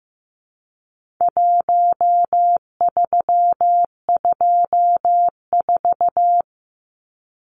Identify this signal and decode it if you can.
Morse code, heard as a keyed tone, about 15 wpm, 700 Hz. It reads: 1324
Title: Numbers - Contest Style - 4-Digit 15wpm